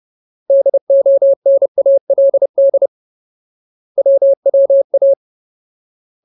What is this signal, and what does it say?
Morse code, heard as a keyed tone, about 30 wpm, 550 Hz. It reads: DONALD WWA
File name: Ultimate Word List / 30WPM Ultimate